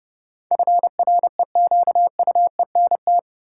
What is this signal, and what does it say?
Morse code, heard as a keyed tone, about 30 wpm, 700 Hz. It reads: FREQUENT
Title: Sentences from Top 1400 Words - New Word Drill Alt - 30wpm